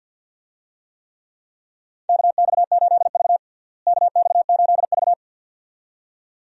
Morse code, heard as a keyed tone, about 53 wpm, 700 Hz. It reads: KX8V KX8V
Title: USA Call Signs - 4 Characters - Repeated 2x - 50wpm